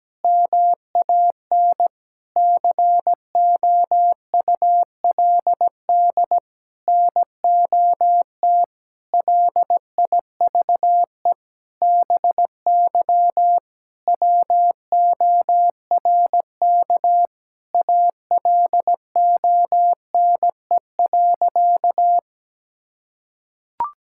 Morse code, heard as a keyed tone, about 17 wpm, 700 Hz. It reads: MAN COULD NOT LIVE BY WORK ALONE.
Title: Sentences from Top 1600 Words 17wpm